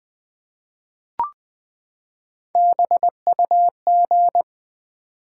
Morse code, heard as a keyed tone, about 20 wpm, 700 Hz. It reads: BUG
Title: Three Letter Words - 20wpm